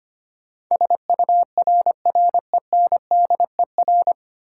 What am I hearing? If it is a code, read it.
Morse code, heard as a keyed tone, about 25 wpm, 700 Hz. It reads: SURRENDER
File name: Nine Letter Words - 25wpm